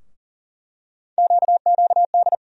Morse code, heard as a keyed tone, about 40 wpm, 700 Hz. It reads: QQD
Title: USA Call Sign Postfixes - 3 Letters 40wpm